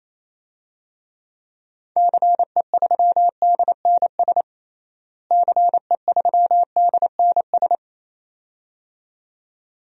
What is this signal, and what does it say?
Morse code, heard as a keyed tone, about 28 wpm, 700 Hz. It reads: CE3DNS CE3DNS
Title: DXCC-100 Call Signs - 6 Characters - Repeated 2x - 28wpm